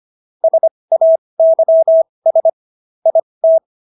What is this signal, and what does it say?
Morse code, heard as a keyed tone, about 25 wpm, spaced as for 20 wpm, 650 Hz. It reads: SAYS IT